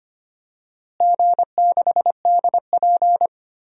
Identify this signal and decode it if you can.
Morse code, heard as a keyed tone, about 25 wpm, 700 Hz. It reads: G6DP
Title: DXCC-100 Call Signs - 4 Characters - Repeated 2x - 25wpm